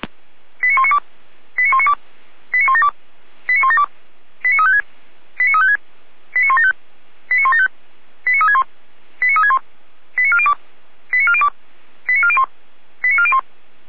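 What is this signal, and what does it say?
Five-tone selective call (ZVEI), heard as quick runs of five tones, tones of about 70 ms. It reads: 89192 89192 89172 89172 89356 89356 89175 89175 89361 89361 89402 89402 89401 89401
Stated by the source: Five-tone ZVEI selective call received on an SDR.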